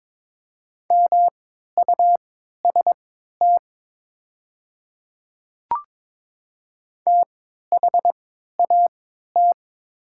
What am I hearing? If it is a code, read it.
Morse code, heard as a keyed tone, about 22 wpm, spaced as for 12 wpm, 700 Hz. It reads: MUST THAT
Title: 3x Character Spacing - Top 300 Words - 22wpm